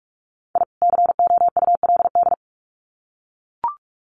Morse code, heard as a keyed tone, about 44 wpm, 700 Hz. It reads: I COULD
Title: Sets of 2 Words 45wpm